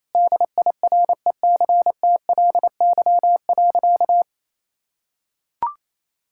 Morse code, heard as a keyed tone, about 28 wpm, 700 Hz. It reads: DIRECTLY.
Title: Sentences from Top 1000 Words - Review Part 5 - 28wpm